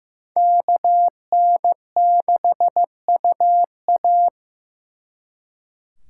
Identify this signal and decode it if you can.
Morse code, heard as a keyed tone, about 18 wpm, 700 Hz. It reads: KN6UA
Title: Speed Racing - USA Call Signs 15wpm